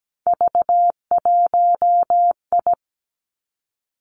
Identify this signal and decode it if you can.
Morse code, heard as a keyed tone, about 17 wpm, 700 Hz. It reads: V1I
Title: Three Challenging Letter-Number Combinations 17wpm